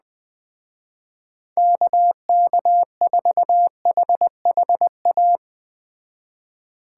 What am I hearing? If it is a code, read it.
Morse code, heard as a keyed tone, about 21 wpm, 700 Hz. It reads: KK4HHA
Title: Speed Racing - USA 2x3 Call Signs 17wpm